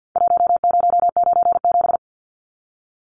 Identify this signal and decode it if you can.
Morse code, heard as a keyed tone, about 49 wpm, 700 Hz. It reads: 1097
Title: Numbers - Contest Style - 4-Digit 50wpm